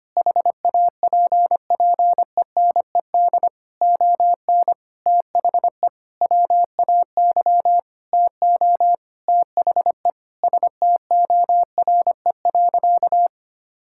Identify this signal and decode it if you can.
Morse code, heard as a keyed tone, about 25 wpm, 700 Hz. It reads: HAPPENED ON THE WAY TO THE STORE.
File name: practice / 25wpm / Sentences with 9-Word Phrases - 25wpm